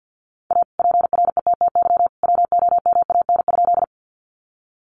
Morse code, heard as a keyed tone, about 50 wpm, 700 Hz. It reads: A PRETTY WOMAN?
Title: Sentences from Top 1100 Words 50wpm